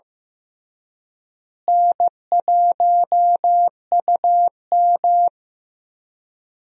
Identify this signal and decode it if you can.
Morse code, heard as a keyed tone, about 15 wpm, 700 Hz. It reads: N1UM
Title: USA Call Signs - 4 Characters - Repeated 2x - 15wpm